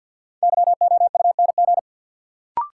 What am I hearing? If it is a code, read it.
Morse code, heard as a keyed tone, about 53 wpm, 700 Hz. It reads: YOUNG
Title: Sentences from Top 400 Words - New Word Drill - 50wpm